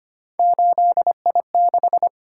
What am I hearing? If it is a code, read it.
Morse code, heard as a keyed tone, about 25 wpm, 700 Hz. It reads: 8I6
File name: Three Letter-Number Combinations - 25wpm